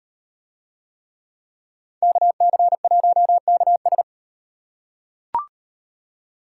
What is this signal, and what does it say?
Morse code, heard as a keyed tone, about 38 wpm, 700 Hz. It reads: KC1KS